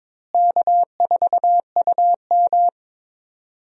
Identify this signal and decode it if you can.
Morse code, heard as a keyed tone, about 22 wpm, 700 Hz. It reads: K4UM
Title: USA Call Signs - 4 Characters - Repeated 3x - 22wpm